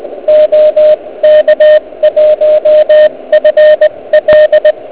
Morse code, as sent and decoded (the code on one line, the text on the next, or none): --- -.- .---- ..-. .-..
OK1FL